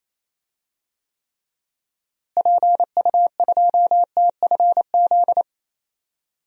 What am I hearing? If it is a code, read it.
Morse code, heard as a keyed tone, about 28 wpm, 700 Hz. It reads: PU2TFZ
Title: DXCC-100 Call Signs - 6 Characters - Repeated 2x - 28wpm